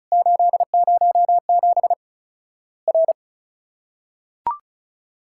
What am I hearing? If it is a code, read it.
Morse code, heard as a keyed tone, about 34 wpm, 700 Hz. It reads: 807 R
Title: Worked All Europe DX Contest - QTC Exchange - 35wpm